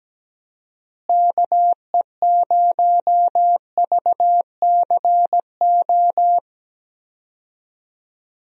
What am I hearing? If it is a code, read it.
Morse code, heard as a keyed tone, about 17 wpm, 700 Hz. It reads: KE0VCO
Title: USA Call Signs - 6 Characters - Repeated 2x - 17wpm